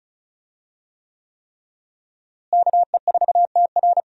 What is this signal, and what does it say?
Morse code, heard as a keyed tone, about 36 wpm, 700 Hz. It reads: KE4TR
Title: USA Call Signs - 5 Characters - Repeated 3x - 35wpm